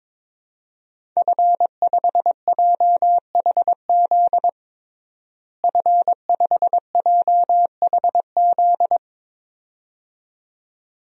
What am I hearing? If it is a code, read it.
Morse code, heard as a keyed tone, about 22 wpm, 700 Hz. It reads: F5JHZ F5JHZ
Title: DXCC-100 Call Signs - 5 Characters - Repeated 2x - 22wpm